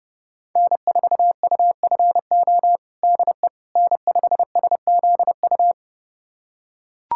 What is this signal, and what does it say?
Morse code, heard as a keyed tone, about 31 wpm, 700 Hz. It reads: N4UFO DE N5SZU
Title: USA Callsign DE Callsign - 5 Characters - 30wpm